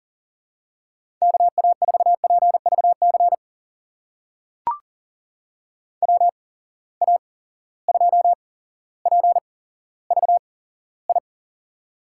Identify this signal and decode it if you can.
Morse code, heard as a keyed tone, about 40 wpm, spaced as for 17 wpm, 700 Hz. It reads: KA4PVC WA2PVI
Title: USA Call Signs - 6 Characters - 8x Character Spacing - 40wpm